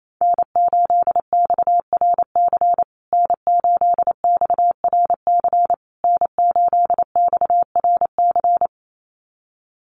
Morse code, heard as a keyed tone, about 28 wpm, 700 Hz. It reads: N8XRC N8XRC N8XRC
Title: USA Call Signs Repeated 3x 28wpm